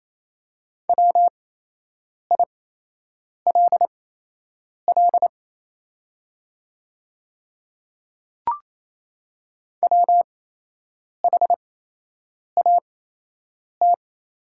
Morse code, heard as a keyed tone, about 28 wpm, spaced as for 8 wpm, 690 Hz. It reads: WILL WHAT
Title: 8x Character Spacing - Top 100 Words - 28wpm